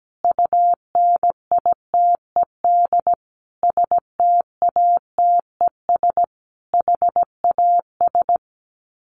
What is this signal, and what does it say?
Morse code, heard as a keyed tone, about 17 wpm, 700 Hz. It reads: UNITED STATES HAS
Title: Sets of 5 Words Encore 17wpm